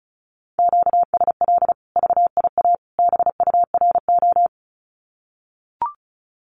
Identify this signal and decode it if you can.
Morse code, heard as a keyed tone, about 34 wpm, 700 Hz. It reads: QSL VIA BURO
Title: Common QSO Phrases 35wpm